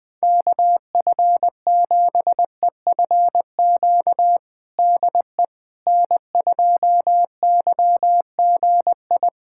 Morse code, heard as a keyed tone, about 20 wpm, 700 Hz. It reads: KF7EFQ DE N2YGI
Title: USA Callsign DE Callsign 20wpm